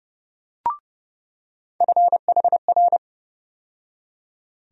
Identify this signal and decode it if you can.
Morse code, heard as a keyed tone, about 30 wpm, 700 Hz. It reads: FHR
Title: USA Call Sign Postfixes - 3 Letters 30wpm